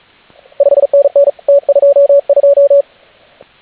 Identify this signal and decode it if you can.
Morse code, heard as a keyed tone, about 36 wpm, 560 Hz. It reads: 5NN T22